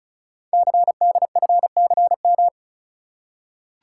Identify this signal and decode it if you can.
Morse code, heard as a keyed tone, about 36 wpm, 700 Hz. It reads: CDFCM
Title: Code Groups - 5 Letters - 35wpm